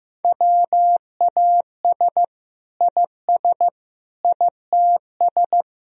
Morse code, heard as a keyed tone, about 15 wpm, 700 Hz. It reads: WAS IS ITS
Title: Sentences from Top 1000 Words - Review Part 2 - 15wpm